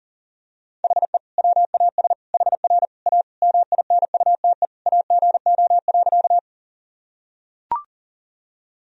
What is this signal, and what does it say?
Morse code, heard as a keyed tone, about 40 wpm, 700 Hz. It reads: HE WAS HR A MINUTE AGO.
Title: Sentences from Top 500 Words and Abbreviations 40wpm